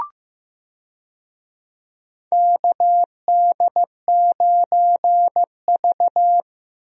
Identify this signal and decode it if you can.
Morse code, heard as a keyed tone, about 16 wpm, 700 Hz. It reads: KD9V
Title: USA Call Signs - 4 Characters - Repeated 2x - 15wpm